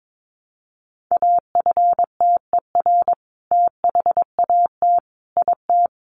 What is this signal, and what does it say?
Morse code, heard as a keyed tone, about 22 wpm, 700 Hz. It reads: AFTER THAT IT
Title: Sentences from Top 1300 Words 22wpm